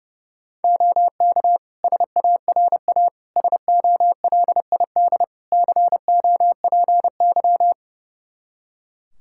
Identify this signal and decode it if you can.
Morse code, heard as a keyed tone, about 30 wpm, 700 Hz. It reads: OK SARA SOLID COPY
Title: Practice QSO - Exchange 3 - More Info - 30wpm